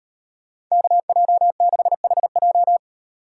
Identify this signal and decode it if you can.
Morse code, heard as a keyed tone, about 37 wpm, 700 Hz. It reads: KJ6HJ